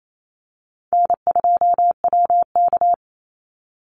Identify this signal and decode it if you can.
Morse code, heard as a keyed tone, about 28 wpm, 700 Hz. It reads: N2WK
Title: Easy Call Signs - 28wpm